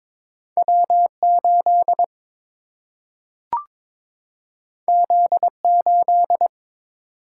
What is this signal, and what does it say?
Morse code, heard as a keyed tone, about 22 wpm, 700 Hz. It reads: W8 Z8
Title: Single-Letter Single-Number 22wpm